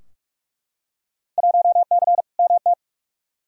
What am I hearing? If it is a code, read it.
Morse code, heard as a keyed tone, about 42 wpm, 700 Hz. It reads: 1C MT
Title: Field Day Exchanges Encore - 45wpm